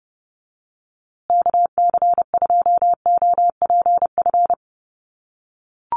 Morse code, heard as a keyed tone, about 30 wpm, 700 Hz. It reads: KC2OPF
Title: USA Call Signs Repeated 2x 30wpm